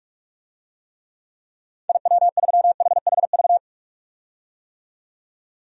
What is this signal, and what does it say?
Morse code, heard as a keyed tone, about 46 wpm, 700 Hz. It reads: IW3HHV
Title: DXCC-100 Call Signs - 6 Characters - 45wpm